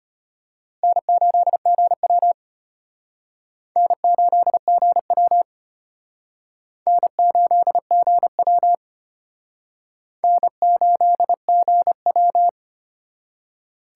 Morse code, heard as a keyed tone, about 32 wpm, 700 Hz. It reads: N8GW N8GW N8GW N8GW